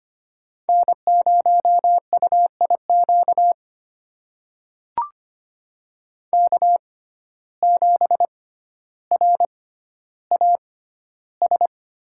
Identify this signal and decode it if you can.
Morse code, heard as a keyed tone, about 25 wpm, spaced as for 13 wpm, 700 Hz. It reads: N0UIQ K7RAS
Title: USA Call Signs - 5 Characters - 6x Character Spacing - 25wpm